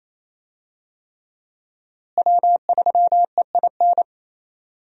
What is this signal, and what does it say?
Morse code, heard as a keyed tone, about 28 wpm, 700 Hz. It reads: W3EIN
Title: USA Call Signs - 5 Characters - Repeated 3x - 28wpm